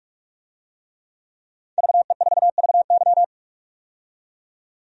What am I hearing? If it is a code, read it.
Morse code, heard as a keyed tone, about 46 wpm, 690 Hz. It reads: VE4VY
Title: DXCC-100 Call Signs - 5 Characters - 45wpm